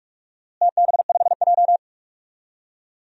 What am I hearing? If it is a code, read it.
Morse code, heard as a keyed tone, about 46 wpm, 700 Hz. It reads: TB5J
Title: DXCC-100 Call Signs - 4 Characters - Repeated 2x - 45wpm